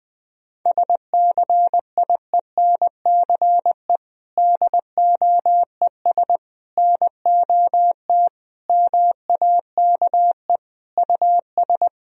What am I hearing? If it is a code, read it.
Morse code, heard as a keyed tone, about 20 wpm, 700 Hz. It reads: SCIENCE DOES NOT MAKE US
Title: Sentences from Top 1000 Words - Review Part 2 - 20wpm